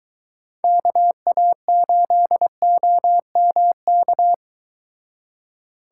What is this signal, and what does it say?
Morse code, heard as a keyed tone, about 22 wpm, 700 Hz. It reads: KA8OMK